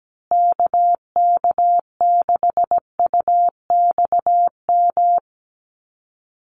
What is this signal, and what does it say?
Morse code, heard as a keyed tone, about 17 wpm, 700 Hz. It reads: KK6UXM
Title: USA Call Signs Repeated 2x 17wpm